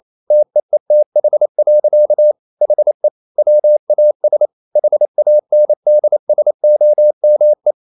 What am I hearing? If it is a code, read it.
Morse code, heard as a keyed tone, about 28 wpm, 600 Hz. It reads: TEETH. HE WAS HANDSOME
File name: Never 0027wpm